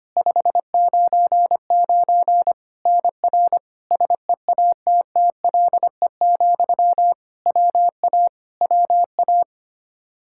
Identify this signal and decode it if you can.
Morse code, heard as a keyed tone, about 25 wpm, 700 Hz. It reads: 599 NR SEATTLE, WA WA
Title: Practice QSO - Exchange 1 - Beginning - 25wpm